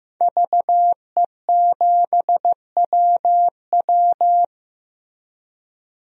Morse code, heard as a keyed tone, about 15 wpm, 700 Hz. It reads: VE7WW
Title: DXCC-100 Call Signs - 5 Characters - Repeated 3x - 15wpm